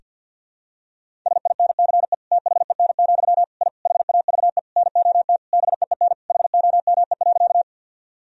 Morse code, heard as a keyed tone, about 47 wpm, 700 Hz. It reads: SINCE THEN, I HAVE NOT BEEN HOME.
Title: Sentences from Top 1000 Words - Review Part 1 - 50wpm